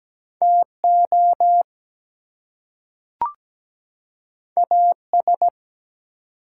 Morse code, heard as a keyed tone, about 17 wpm, 700 Hz. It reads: TO AS
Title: Two Letter Words - 17wpm